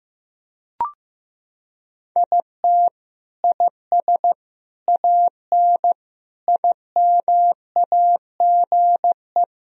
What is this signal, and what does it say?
Morse code, heard as a keyed tone, about 15 wpm, 700 Hz. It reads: IT IS AN IMAGE
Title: Sentences from Top 1000 Words - Review Part 4 - 15wpm